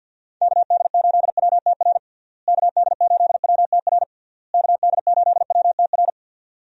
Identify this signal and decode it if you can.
Morse code, heard as a keyed tone, about 51 wpm, 700 Hz. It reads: KD8WTR KD8WTR KD8WTR
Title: USA Call Signs - 6 Characters - Repeated 3x - 50wpm